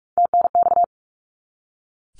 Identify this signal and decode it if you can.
Morse code, heard as a keyed tone, about 44 wpm, 700 Hz. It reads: TNX
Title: Abbreviations 45wpm